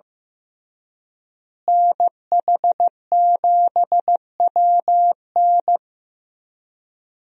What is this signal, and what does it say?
Morse code, heard as a keyed tone, about 15 wpm, 700 Hz. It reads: NH7WN
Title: USA Call Signs - 5 Characters - Repeated 2x - 15wpm